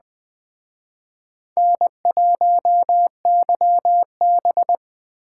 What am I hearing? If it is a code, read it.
Morse code, heard as a keyed tone, about 20 wpm, 700 Hz. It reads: N1YB